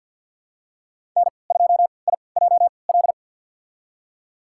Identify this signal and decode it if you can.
Morse code, heard as a keyed tone, about 53 wpm, spaced as for 29 wpm, 700 Hz. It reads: N2IJL
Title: USA Call Signs - 5 Characters - 3x Character Spacing - 50wpm